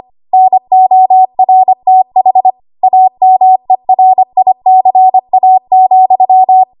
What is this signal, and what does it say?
Morse code, heard as a keyed tone, about 25 wpm, 750 Hz. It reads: NORTH AMERICA,
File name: Archive / 25 WPM / 25WPM